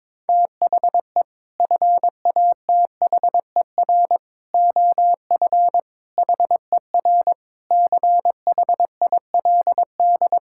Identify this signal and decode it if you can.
Morse code, heard as a keyed tone, about 21 wpm, 700 Hz. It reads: THE FATHER OF HER CHILD
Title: Sentences from Top 1000 Words - Review Part 5 - 22wpm